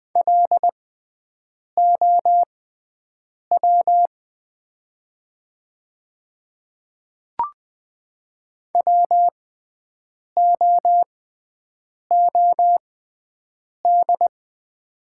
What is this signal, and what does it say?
Morse code, heard as a keyed tone, about 20 wpm, spaced as for 7 wpm, 700 Hz. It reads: LOW WOOD
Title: 6x Character Spacing - Top 300 Words - 20wpm